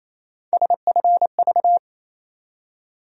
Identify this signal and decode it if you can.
Morse code, heard as a keyed tone, about 28 wpm, 700 Hz. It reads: SFV
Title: USA Call Sign Postfixes - 3 Letters 28wpm